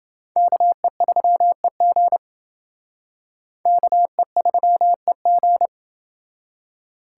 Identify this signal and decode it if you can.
Morse code, heard as a keyed tone, about 28 wpm, 700 Hz. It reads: KE3EG KE3EG